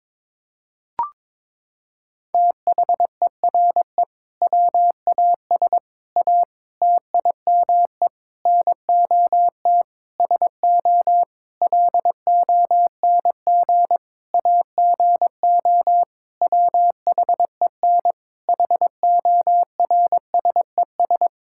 Morse code, heard as a keyed tone, about 22 wpm, 700 Hz. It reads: THERE WAS A TIME NOT SO LONG AGO WHEN HORSES
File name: practice / 22wpm / Sentences with 8-Word Phrases - 22wpm